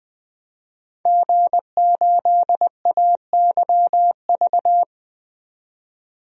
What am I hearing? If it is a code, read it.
Morse code, heard as a keyed tone, about 20 wpm, 700 Hz. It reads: G8AYV
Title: DXCC-100 Call Signs - 5 Characters - 20wpm